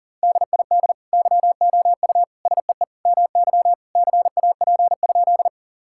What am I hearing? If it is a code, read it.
Morse code, heard as a keyed tone, about 40 wpm, 700 Hz. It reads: DID YOU SEE MY CAP?
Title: Sentences from Top 1600 Words 40wpm